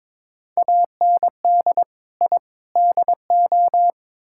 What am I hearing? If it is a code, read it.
Morse code, heard as a keyed tone, about 21 wpm, 700 Hz. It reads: AND I DO
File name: practice / 22wpm / Build A Phrase - 5-Word Phrases - 22wpm